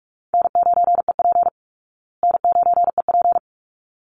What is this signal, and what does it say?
Morse code, heard as a keyed tone, about 44 wpm, 700 Hz. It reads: N9EP N9EP
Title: Easy Call Signs Repeated 2x - 45wpm